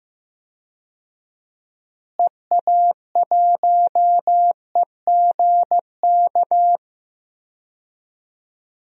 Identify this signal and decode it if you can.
Morse code, heard as a keyed tone, about 14 wpm, 700 Hz. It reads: EA1EGK
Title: DXCC-100 Call Signs - 6 Characters - Repeated 2x - 15wpm